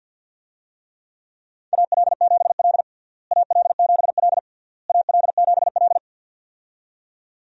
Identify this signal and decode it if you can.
Morse code, heard as a keyed tone, about 52 wpm, 700 Hz. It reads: AL7L AL7L AL7L
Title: USA Call Signs - 4 Characters - Repeated 3x - 50wpm